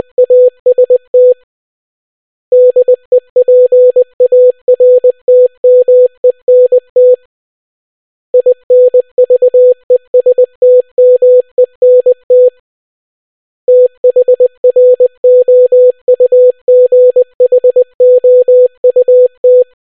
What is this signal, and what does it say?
Morse code, heard as a keyed tone, about 20 wpm, 500 Hz. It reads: AST DEPARTMENT INVESTMENT THROUGHOUT